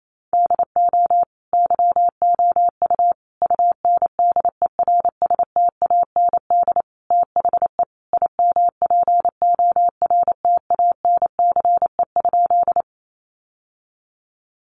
Morse code, heard as a keyed tone, about 28 wpm, 700 Hz. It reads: DO YOU UNDERSTAND THE IMPORTANCE?
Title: Sentences from Top 1200 Words 28wpm